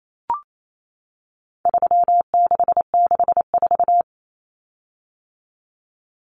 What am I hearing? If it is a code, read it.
Morse code, heard as a keyed tone, about 28 wpm, 700 Hz. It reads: 3664
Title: Common Cut Numbers - 4-Digit 28wpm